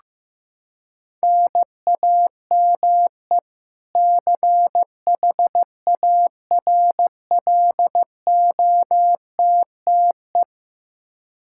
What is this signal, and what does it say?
Morse code, heard as a keyed tone, about 15 wpm, 700 Hz. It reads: NAME CHARLOTTE
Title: Practice QSO - Exchange 1 - Beginning - 15wpm